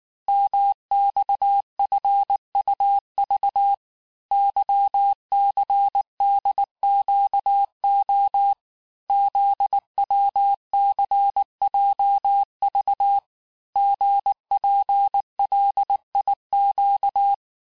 Morse code, heard as a keyed tone, about 19 wpm, 800 Hz. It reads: MXFUV YCDQO ZWCJV GPLIQ
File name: cours-du-lundi files / cours-1583--texte-chiffre-18-mpm